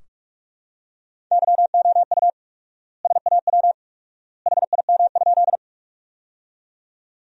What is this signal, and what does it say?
Morse code, heard as a keyed tone, about 42 wpm, 700 Hz. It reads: YOU SAW HIM?
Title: 4x Word-Spacing - Sentences from Top 300 Words - 45wpm